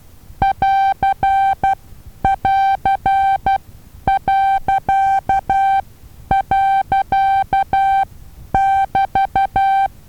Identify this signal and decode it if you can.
Morse code, heard as a keyed tone, about 12 wpm, spaced as for 9 wpm, 780 Hz. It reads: <AR><AR>..<BT>
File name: Taviro HA6VA B